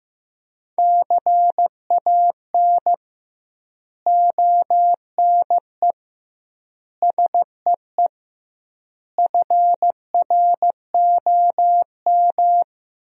Morse code, heard as a keyed tone, about 15 wpm, 700 Hz. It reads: CAN ONE SEE FROM
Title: 2x Word-Spacing - Sentences from Top 100 Words - 15wpm